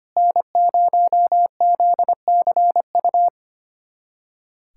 Morse code, heard as a keyed tone, about 25 wpm, 700 Hz. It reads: N0ZCU
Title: USA Callsign DE Callsign 25wpm